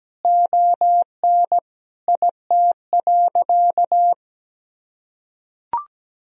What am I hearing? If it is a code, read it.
Morse code, heard as a keyed tone, about 17 wpm, 700 Hz. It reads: ON IT.
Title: Sentences from Top 1000 Words - Review Part 1 - 17wpm